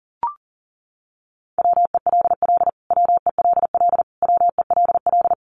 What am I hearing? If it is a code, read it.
Morse code, heard as a keyed tone, about 40 wpm, 700 Hz. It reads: WELL WELL WELL
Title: Top 500 Words Repeated 40wpm